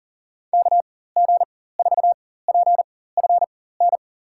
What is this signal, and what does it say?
Morse code, heard as a keyed tone, about 38 wpm, spaced as for 18 wpm, 700 Hz. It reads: KG4PFN
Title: USA Call Signs - 6 Characters - 4x Character Spacing - 40wpm